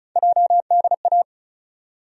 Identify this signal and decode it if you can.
Morse code, heard as a keyed tone, about 36 wpm, 700 Hz. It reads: JDA
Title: USA Call Sign Postfixes - 3 Letters 35wpm